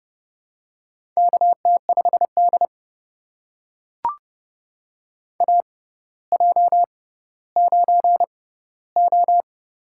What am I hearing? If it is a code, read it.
Morse code, heard as a keyed tone, about 30 wpm, spaced as for 15 wpm, 700 Hz. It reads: KT5D AJ9O
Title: USA Call Signs - 4 Characters - 6x Character Spacing - 30wpm